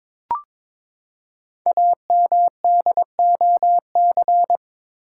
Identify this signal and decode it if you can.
Morse code, heard as a keyed tone, about 22 wpm, 700 Hz. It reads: AMDOC
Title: Code Groups - 5 Letters - 22wpm